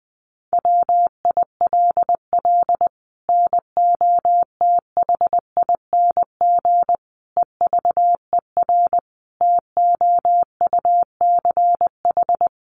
Morse code, heard as a keyed tone, about 20 wpm, 700 Hz. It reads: WILL NOTHING EVER TOUCH
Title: Sentences from Top 1200 Words 20wpm